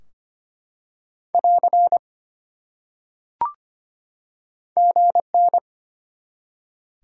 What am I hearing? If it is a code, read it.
Morse code, heard as a keyed tone, about 25 wpm, 700 Hz. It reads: <AR> GN
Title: Top 180 QSO Elements 25wpm